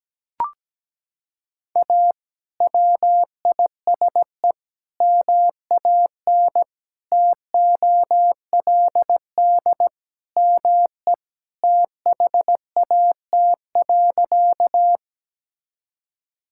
Morse code, heard as a keyed tone, about 17 wpm, 700 Hz. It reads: A WISE MAN TOLD ME THAT.
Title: Sentences from Top 1600 Words 17wpm